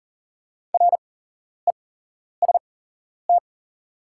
Morse code, heard as a keyed tone, about 37 wpm, spaced as for 11 wpm, 700 Hz. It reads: REST
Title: 8x Character Spacing - Top 500 Words - 40wpm